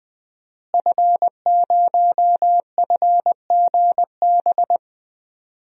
Morse code, heard as a keyed tone, about 20 wpm, 700 Hz. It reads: F0FGB
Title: DXCC-100 Call Signs - 5 Characters - Repeated 3x - 20wpm